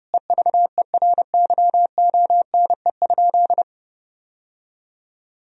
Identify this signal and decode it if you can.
Morse code, heard as a keyed tone, about 32 wpm, 700 Hz. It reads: EVERYONE?
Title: Sentences from Top 1000 Words - Review Part 4 - 30wpm